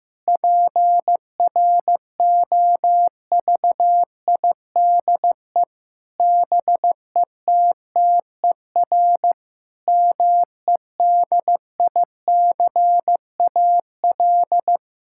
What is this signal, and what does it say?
Morse code, heard as a keyed tone, about 15 wpm, 700 Hz. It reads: PROVIDE BETTER MEDICAL
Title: Sentences from Top 1000 Words - Review Part 5 - 15wpm